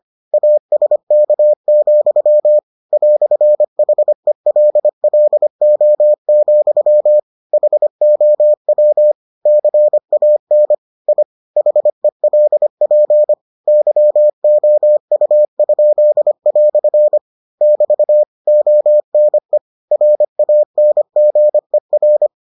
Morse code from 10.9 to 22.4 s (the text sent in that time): I HELP YOU?" <BT> ONE RANGER